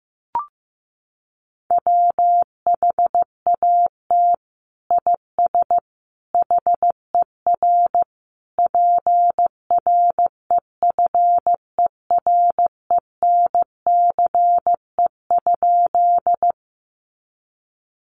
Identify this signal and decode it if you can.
Morse code, heard as a keyed tone, about 15 wpm, 700 Hz. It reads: WHAT IS HER PREFERENCE?
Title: Sentences from Top 1500 Words 15wpm